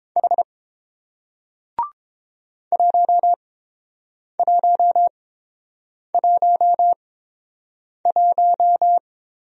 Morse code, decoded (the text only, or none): H 1111